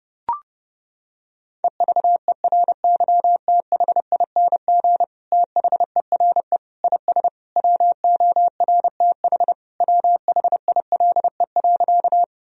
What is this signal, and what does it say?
Morse code, heard as a keyed tone, about 30 wpm, 700 Hz. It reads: EVERYTHING THERE IS WORTH WHILE.
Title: Sentences from Top 1000 Words - Review Part 1 - 30wpm